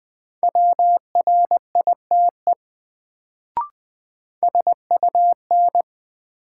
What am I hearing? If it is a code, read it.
Morse code, heard as a keyed tone, about 20 wpm, 700 Hz. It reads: WRITE SUN
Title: Sentences from Top 300 Words - New Word Drill - 20wpm